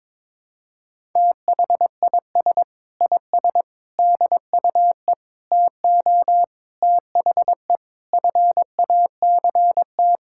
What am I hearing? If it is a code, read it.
Morse code, heard as a keyed tone, about 22 wpm, 700 Hz. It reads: THIS IS DUE TO THE FACT
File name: practice / 22wpm / Sets of 7 Words - 22wpm